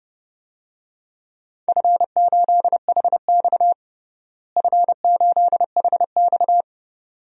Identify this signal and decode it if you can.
Morse code, heard as a keyed tone, about 32 wpm, 700 Hz. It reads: F8HX F8HX
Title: DXCC-100 Call Signs - 4 Characters - Repeated 2x - 30wpm